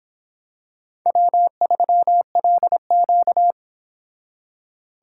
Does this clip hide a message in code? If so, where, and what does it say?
Yes: Morse code from 1.1 to 3.5 s, heard as a keyed tone, about 26 wpm, 700 Hz. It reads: W3LQ